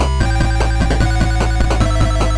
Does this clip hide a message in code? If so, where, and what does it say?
No.